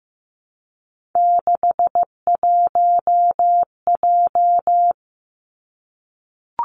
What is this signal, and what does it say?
Morse code, heard as a keyed tone, about 15 wpm, 700 Hz. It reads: 61J
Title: Three Challenging Letter-Number Combinations 15wpm